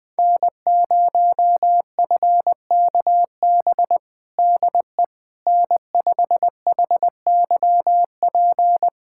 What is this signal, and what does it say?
Morse code, heard as a keyed tone, about 20 wpm, 700 Hz. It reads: N0FKB DE N5HYP
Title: USA Callsign DE Callsign - 5 Characters - 20wpm